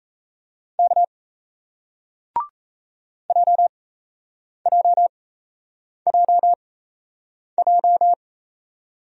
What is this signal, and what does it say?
Morse code, heard as a keyed tone, about 38 wpm, spaced as for 8 wpm, 700 Hz. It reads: K JJJJ